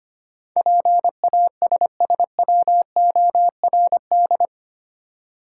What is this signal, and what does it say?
Morse code, heard as a keyed tone, about 25 wpm, 700 Hz. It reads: PASSWORD
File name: Eight Letter Words - 25wpm